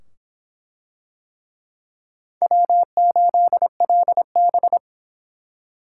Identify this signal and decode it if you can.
Morse code, heard as a keyed tone, about 25 wpm, 700 Hz. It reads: W8LB